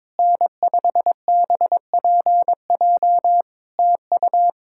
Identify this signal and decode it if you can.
Morse code, heard as a keyed tone, about 22 wpm, 700 Hz. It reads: N5BPJ TU
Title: K1USN SST QSO Practice 22wpm